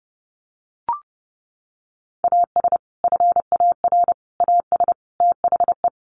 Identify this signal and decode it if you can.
Morse code, heard as a keyed tone, about 30 wpm, 700 Hz. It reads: AS FAR AS THE
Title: Sets of 4 Words 30wpm